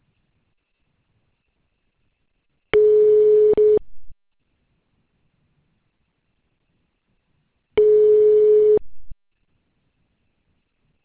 Ringback tone (Europe, 425 Hz).